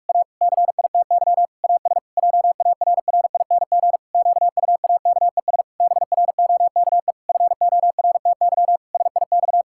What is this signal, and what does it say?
Morse code, heard as a keyed tone, about 41 wpm, 700 Hz. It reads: A CITY AS JARRING QUAKES BROKE FORTY SIX
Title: Pangram Challenge v2 - 45wpm